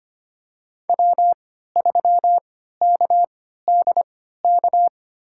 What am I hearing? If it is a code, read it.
Morse code, heard as a keyed tone, about 25 wpm, spaced as for 14 wpm, 700 Hz. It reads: W3KDK
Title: USA Call Signs - 5 Characters - 3x Character Spacing - 25wpm